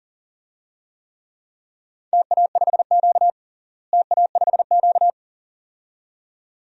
Morse code, heard as a keyed tone, about 40 wpm, 700 Hz. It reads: TA5Q TA5Q
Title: DXCC-100 Call Signs - 4 Characters - Repeated 2x - 40wpm